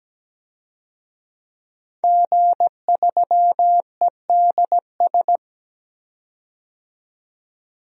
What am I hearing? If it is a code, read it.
Morse code, heard as a keyed tone, about 17 wpm, 700 Hz. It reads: G3EDS
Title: DXCC-100 Call Signs - 5 Characters - 17wpm